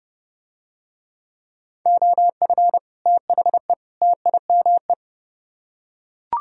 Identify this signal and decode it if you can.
Morse code, heard as a keyed tone, about 31 wpm, 700 Hz. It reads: OF THE TIME